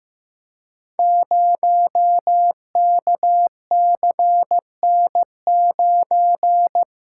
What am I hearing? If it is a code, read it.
Morse code, heard as a keyed tone, about 15 wpm, 700 Hz. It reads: 0KCN9
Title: Code Groups - 5 Letter:Numbers - 15wpm